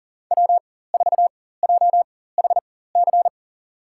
Morse code, heard as a keyed tone, about 37 wpm, spaced as for 18 wpm, 700 Hz. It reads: W4JHC
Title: USA Call Signs - 5 Characters - 4x Character Spacing - 40wpm